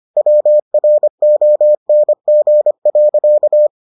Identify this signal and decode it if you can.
Morse code, heard as a keyed tone, about 25 wpm, 600 Hz. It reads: WRONG.